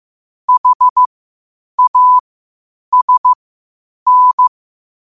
Morse code, heard as a keyed tone, about 15 wpm, spaced as for 8 wpm, 1000 Hz. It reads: HASN
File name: qotdfiles / 8wpm